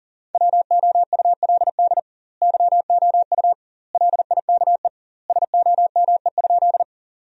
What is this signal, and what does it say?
Morse code, heard as a keyed tone, about 40 wpm, 700 Hz. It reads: WOULD YOU LIKE SOME?
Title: 2x Word-Spacing - Sentences from Top 100 Words - 40wpm